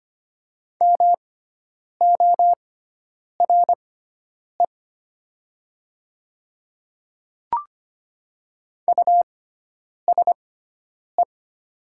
Morse code, heard as a keyed tone, about 25 wpm, spaced as for 9 wpm, 700 Hz. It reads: MORE USE
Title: 6x Character Spacing - Top 200 Words - 25wpm